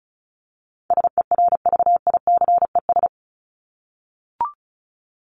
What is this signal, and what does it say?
Morse code, heard as a keyed tone, about 34 wpm, 700 Hz. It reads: SERVICES
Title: ES:S Suffix - Top 100 Words with Base Words 35wpm